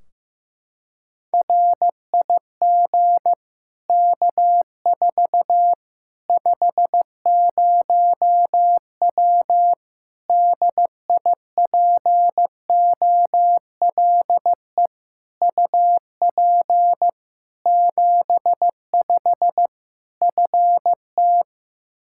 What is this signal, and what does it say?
Morse code, heard as a keyed tone, about 15 wpm, 700 Hz. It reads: RIG K4 50W DIPOLE UP 75 FT
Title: Practice QSO - Exchange 2 - Info - 15wpm